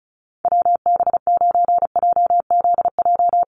Morse code, heard as a keyed tone, about 34 wpm, 700 Hz. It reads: WB9JZJ
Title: USA Call Signs Repeated 3x 35wpm